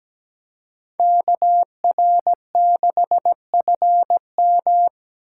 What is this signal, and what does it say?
Morse code, heard as a keyed tone, about 17 wpm, 700 Hz. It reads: KR6FM
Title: USA Call Signs - 5 Characters - 3x Character Spacing - 17wpm